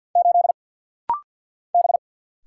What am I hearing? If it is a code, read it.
Morse code, heard as a keyed tone, about 50 wpm, 700 Hz. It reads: 8 B
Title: Single Letter-Number Alt - Rapid-Fire 50wpm